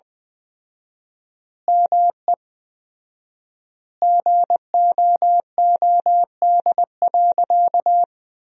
Morse code, heard as a keyed tone, about 20 wpm, 700 Hz. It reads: ME GOOD.
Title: 4x Word-Spacing - Sentences from Top 500 Words - 20wpm